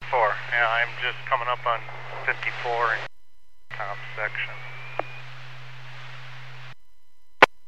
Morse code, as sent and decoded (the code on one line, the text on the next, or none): none